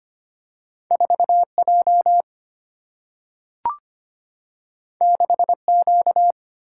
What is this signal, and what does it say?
Morse code, heard as a keyed tone, about 25 wpm, 700 Hz. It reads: 4 J 6 Q
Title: Single-Letter Single-Number 25wpm